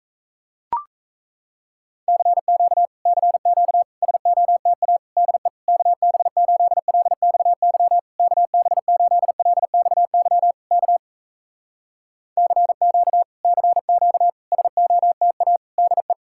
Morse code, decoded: CQ CQ SOTA DE KB8LXY KB8LXY K CQ CQ SOTA DE

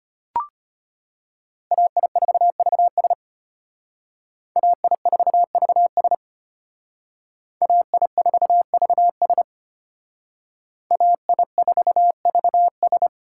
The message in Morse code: AI4VS AI4VS AI4VS AI4VS

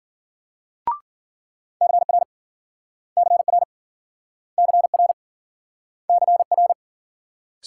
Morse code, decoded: CR CR CR CR

